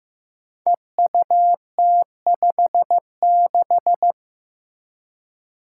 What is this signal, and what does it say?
Morse code, heard as a keyed tone, about 15 wpm, 700 Hz. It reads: EUT56
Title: Code Groups - 5 Letter:Numbers - 15wpm